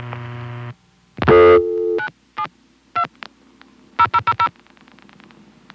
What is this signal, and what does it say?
DTMF, heard as telephone touch tones, tones of about 70 ms. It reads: 97 2 0000